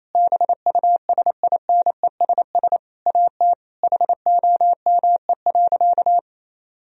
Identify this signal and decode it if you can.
Morse code, heard as a keyed tone, about 28 wpm, 700 Hz. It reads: BUSINESS AT HOME.
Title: Sentences from Top 1000 Words - Review Part 4 - 28wpm